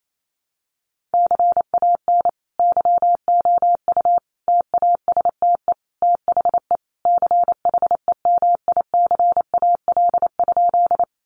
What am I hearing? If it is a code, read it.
Morse code, heard as a keyed tone, about 28 wpm, 700 Hz. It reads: CAN YOU TASTE THE CHEMICAL?
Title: Sentences from Top 1200 Words 28wpm